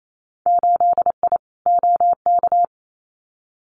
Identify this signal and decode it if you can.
Morse code, heard as a keyed tone, about 28 wpm, 700 Hz. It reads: 8I OK
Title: Winter Field Day Exchanges Encore 28wpm